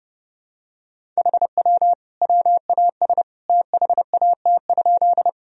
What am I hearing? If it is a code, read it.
Morse code, heard as a keyed tone, about 32 wpm, 700 Hz. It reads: HW WAS THAT?
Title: Sentences from Top 100 Words and Abbreviations 30wpm